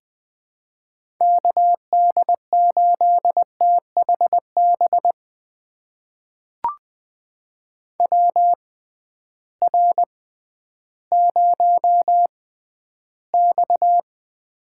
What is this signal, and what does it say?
Morse code, heard as a keyed tone, about 20 wpm, spaced as for 12 wpm, 700 Hz. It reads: KD8THB WR0X
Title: USA Call Signs - 6x Character Spacing - 20wpm